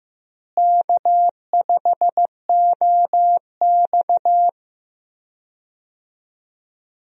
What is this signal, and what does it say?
Morse code, heard as a keyed tone, about 15 wpm, 700 Hz. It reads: K5OX
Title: USA Call Signs - 4 Characters - Repeated 3x - 15wpm